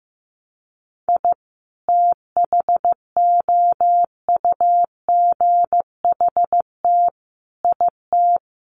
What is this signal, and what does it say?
Morse code, heard as a keyed tone, about 15 wpm, 700 Hz. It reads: I THOUGHT IT
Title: Sets of 4 Words 15wpm